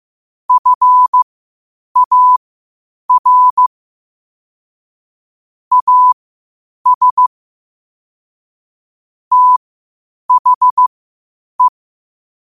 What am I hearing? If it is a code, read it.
Morse code, heard as a keyed tone, about 15 wpm, spaced as for 8 wpm, 1000 Hz. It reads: FAR AS THE
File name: qotdfiles / 8wpm